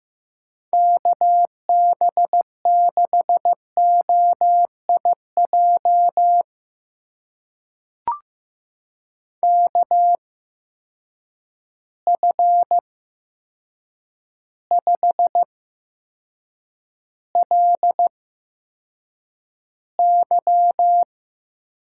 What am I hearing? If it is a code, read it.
Morse code, heard as a keyed tone, about 15 wpm, spaced as for 7 wpm, 700 Hz. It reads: KB6OIJ KF5LY
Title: USA Call Signs - 6 Characters - 8x Character Spacing - 15wpm